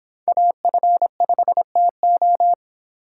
Morse code, heard as a keyed tone, about 26 wpm, 700 Hz. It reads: AF5TO